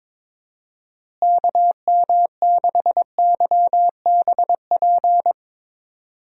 Morse code, heard as a keyed tone, about 22 wpm, 700 Hz. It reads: KM6YBP
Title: USA Call Signs - 6 Characters - Repeated 3x - 22wpm